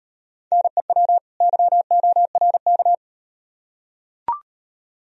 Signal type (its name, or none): Morse code